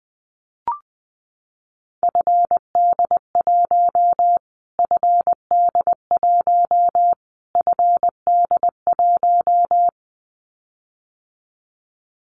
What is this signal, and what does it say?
Morse code, heard as a keyed tone, about 20 wpm, 700 Hz. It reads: FD1 FD1 FD1
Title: Callsign Prefixes for Top 25 DXCC - Repeated 3x - 20wpm